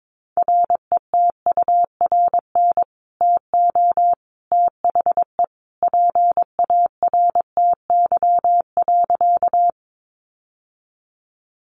Morse code, heard as a keyed tone, about 22 wpm, 700 Hz. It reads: RETURN TO THE PARTY.
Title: Sentences from Top 1400 Words 22wpm